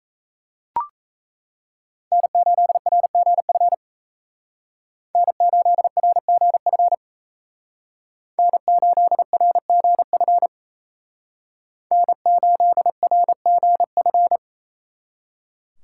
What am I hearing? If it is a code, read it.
Morse code, heard as a keyed tone, about 34 wpm, 700 Hz. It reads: N8RGF N8RGF N8RGF N8RGF
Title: Speed Racing - USA 1x3 Call Signs 28wpm